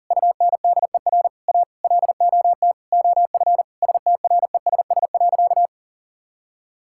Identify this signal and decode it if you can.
Morse code, heard as a keyed tone, about 37 wpm, 700 Hz. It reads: UNDER A LOT OF STRESS.
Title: Sentences from Top 1000 Words - Review Part 6 - 40wpm